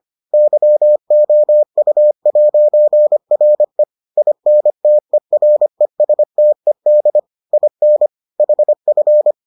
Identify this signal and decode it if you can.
Morse code, heard as a keyed tone, about 25 wpm, 600 Hz. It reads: YOU'RE INTERESTED IN HF